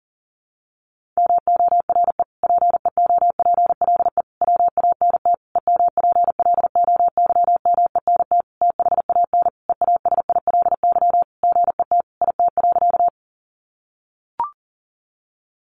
Morse code, heard as a keyed tone, about 40 wpm, 700 Hz. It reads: MORE PEOPLE WANT EMPLOYMENT THAN EASILY GET IT.
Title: Sentences from Top 1500 Words 40wpm